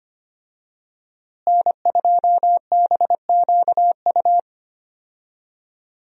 Morse code, heard as a keyed tone, about 25 wpm, 700 Hz. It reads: N2BQU
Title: USA Call Signs - 5 Characters - Repeated 3x - 25wpm